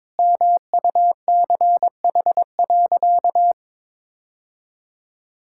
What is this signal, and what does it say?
Morse code, heard as a keyed tone, about 22 wpm, 700 Hz. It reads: MUCH.
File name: practice / 22wpm / Sentences with 2-Word Phrases - 22wpm